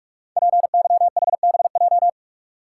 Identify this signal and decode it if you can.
Morse code, heard as a keyed tone, about 44 wpm, 700 Hz. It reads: PYHBJ
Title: Code Groups - 5 Letters - 45wpm